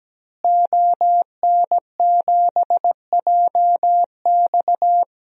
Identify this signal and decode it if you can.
Morse code, heard as a keyed tone, about 17 wpm, 700 Hz. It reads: ON7JX
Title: DXCC-100 Call Signs - 5 Characters - Repeated 3x - 17wpm